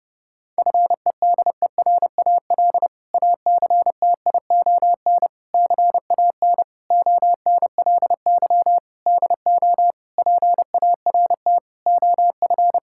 Morse code, heard as a keyed tone, about 30 wpm, 700 Hz. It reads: FEDERAL ACTION CAN ONLY DO PART OF
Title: Sentences from Top 1000 Words - Review Part 5 - 30wpm